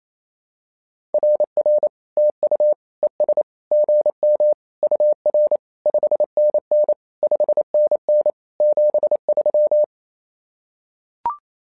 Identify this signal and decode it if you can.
Morse code, heard as a keyed tone, about 27 wpm, 600 Hz. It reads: RR TU ES GM UR 5NN 5NN 73
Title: SOTA QSO Practice 28wpm